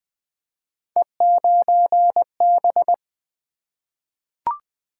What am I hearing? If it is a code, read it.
Morse code, heard as a keyed tone, about 20 wpm, 700 Hz. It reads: E9B
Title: Three Letter-Number Combinations - 20wpm